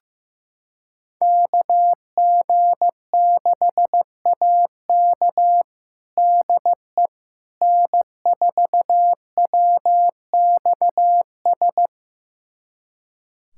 Morse code, heard as a keyed tone, about 15 wpm, 700 Hz. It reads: KG6AK DE N4WXS
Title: USA Callsign DE Callsign - 5 Characters - 15wpm